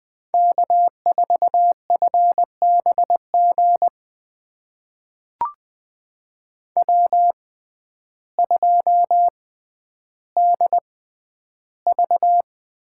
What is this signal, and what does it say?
Morse code, heard as a keyed tone, about 21 wpm, spaced as for 11 wpm, 700 Hz. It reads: K4FBG W2DV
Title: USA Call Signs - 5 Characters - 6x Character Spacing - 20wpm